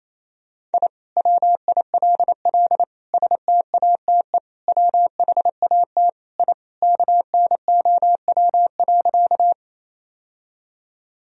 Morse code, heard as a keyed tone, about 28 wpm, 700 Hz. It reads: I WILL STATE WHAT I KNOW.
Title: Sentences from Top 1000 Words - Review Part 2 - 28wpm